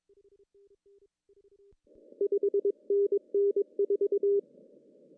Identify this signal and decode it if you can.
Morse code, heard as a keyed tone, about 22 wpm, 400 Hz. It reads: 5NN 4
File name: CW / cqww-cw-k5tr